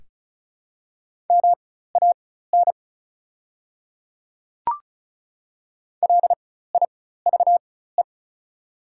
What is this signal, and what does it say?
Morse code, heard as a keyed tone, about 36 wpm, spaced as for 16 wpm, 700 Hz. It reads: MAN LIVE
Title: 4x Character Spacing - Top 200 Words - 35wpm